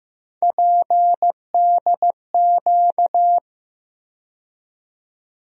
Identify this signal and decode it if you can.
Morse code, heard as a keyed tone, about 15 wpm, 700 Hz. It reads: PDQ
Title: USA Call Sign Postfixes - 3 Letters 15wpm